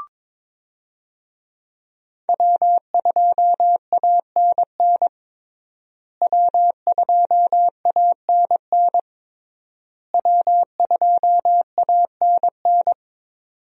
Morse code, heard as a keyed tone, about 22 wpm, 700 Hz. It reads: W2ANN W2ANN W2ANN
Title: USA Call Signs - 5 Characters - Repeated 3x - 22wpm